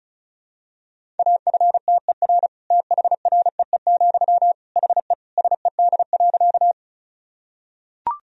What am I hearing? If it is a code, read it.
Morse code, heard as a keyed tone, about 34 wpm, 690 Hz. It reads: AFTER THREE, HE SED.
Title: Sentences from Top 200 Words and Abbreviations 35wpm